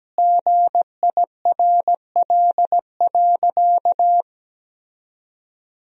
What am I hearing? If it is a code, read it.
Morse code, heard as a keyed tone, about 17 wpm, 700 Hz. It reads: GIRL.
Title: 4x Word-Spacing - Sentences from Top 400 Words - 17wpm